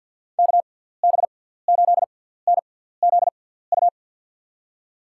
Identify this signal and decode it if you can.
Morse code, heard as a keyed tone, about 50 wpm, spaced as for 17 wpm, 700 Hz. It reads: KB8NZU
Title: USA Call Signs - 6x Character Spacing - 50wpm